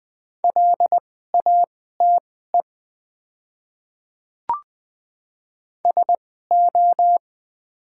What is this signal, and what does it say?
Morse code, heard as a keyed tone, about 20 wpm, spaced as for 14 wpm, 700 Hz. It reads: LATE SO
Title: 2x Character Spacing - Top 300 Words - 20wpm